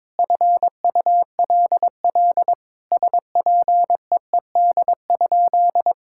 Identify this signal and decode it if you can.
Morse code, heard as a keyed tone, about 22 wpm, 700 Hz. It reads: FULL SPEED?
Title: Sentences from Top 1000 Words - Review Part 5 - 22wpm